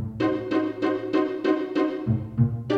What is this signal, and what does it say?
There is no code.